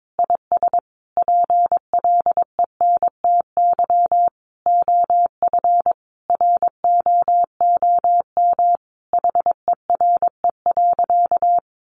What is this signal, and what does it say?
Morse code, heard as a keyed tone, about 22 wpm, 700 Hz. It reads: IS PLENTY OF ROOM HERE.
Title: Sentences from Top 1300 Words 22wpm